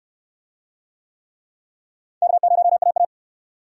Morse code, heard as a keyed tone, about 68 wpm, 700 Hz. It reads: K0AA